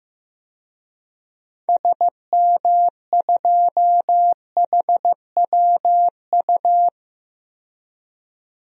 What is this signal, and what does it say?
Morse code, heard as a keyed tone, about 15 wpm, 690 Hz. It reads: SM2HWU
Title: DXCC-100 Call Signs - 6 Characters - 15wpm